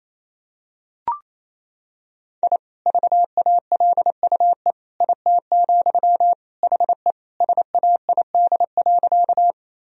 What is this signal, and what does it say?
Morse code, heard as a keyed tone, about 28 wpm, 700 Hz. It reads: I VALUE IT, HE SAID.
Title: Sentences from Top 1000 Words - Review Part 3 - 28wpm